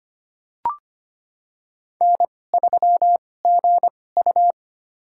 Morse code, heard as a keyed tone, about 25 wpm, spaced as for 18 wpm, 700 Hz. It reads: N3GU
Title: USA Call Signs - 4 Characters - 2x Character Spacing - 25wpm